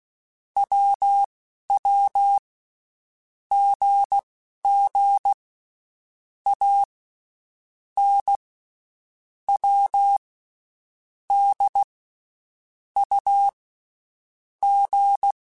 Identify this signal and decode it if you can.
Morse code, heard as a keyed tone, about 16 wpm, spaced as for 11 wpm, 800 Hz. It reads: WW GG A N W D U G
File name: Basic Letter Groups By7Groups / AWUNDG / AWUNDG@10